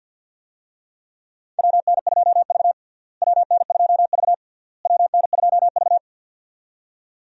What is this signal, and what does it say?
Morse code, heard as a keyed tone, about 51 wpm, 700 Hz. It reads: WN2V WN2V WN2V
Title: USA Call Signs - 4 Characters - Repeated 3x - 50wpm